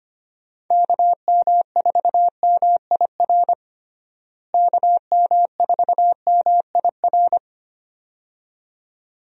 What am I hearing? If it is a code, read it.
Morse code, heard as a keyed tone, about 25 wpm, 700 Hz. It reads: KM4MIR KM4MIR
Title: USA Call Signs - 6 Characters - Repeated 2x - 25wpm